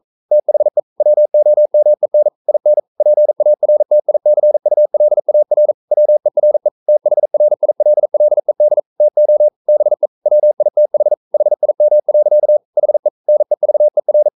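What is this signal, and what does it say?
Morse code, heard as a keyed tone, about 42 wpm, 600 Hz. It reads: THE WOMEN IN PARTICULAR WERE THRILLED TO BE WITH HIM. HE NEVER
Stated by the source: Morse code practice at 41 wpm, from CW med Gnister.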